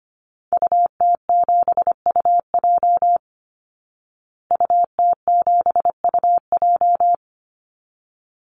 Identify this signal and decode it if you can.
Morse code, heard as a keyed tone, about 25 wpm, 700 Hz. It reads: UT7UJ UT7UJ
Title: Call Signs Repeated 2x - 25wpm